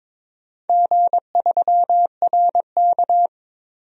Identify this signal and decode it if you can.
Morse code, heard as a keyed tone, about 21 wpm, 700 Hz. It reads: G3RK
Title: DXCC-100 Call Signs - 4 Characters - 22wpm